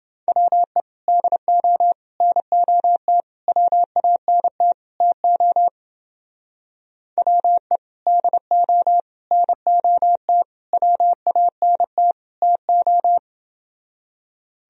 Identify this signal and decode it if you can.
Morse code, heard as a keyed tone, about 29 wpm, 700 Hz. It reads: WE DO NOT WANT TO WE DO NOT WANT TO